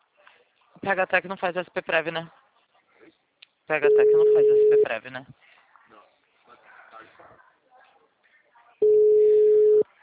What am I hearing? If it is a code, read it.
Ringback tone (Europe, 425 Hz), with speech.